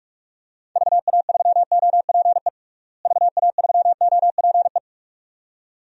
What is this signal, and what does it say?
Morse code, heard as a keyed tone, about 46 wpm, 700 Hz. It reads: VA3OPE VA3OPE
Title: DXCC-100 Call Signs - 6 Characters - Repeated 2x - 45wpm